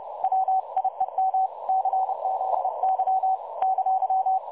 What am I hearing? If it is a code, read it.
Morse code, heard as a keyed tone, about 31 wpm, 760 Hz. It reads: WIEM CZY !
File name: dlazaawansowanych / nagraniazpasma / websdr recording 3565.0kHz